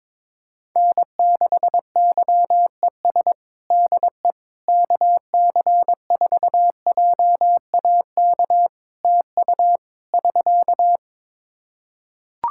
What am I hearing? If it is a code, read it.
Morse code, heard as a keyed tone, about 22 wpm, 700 Hz. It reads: N6YES DE KC4JAK TU <SK>
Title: K1USN SST QSO Practice 22wpm